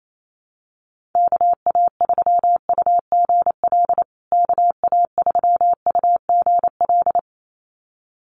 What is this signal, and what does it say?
Morse code, heard as a keyed tone, about 28 wpm, 700 Hz. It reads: KA3UGL KA3UGL
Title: USA Call Signs Repeated 2x 28wpm